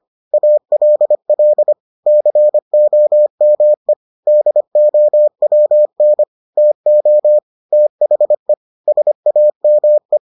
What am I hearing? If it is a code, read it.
Morse code, heard as a keyed tone, about 25 wpm, 600 Hz. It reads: ALL COME DOWN TO THE SAME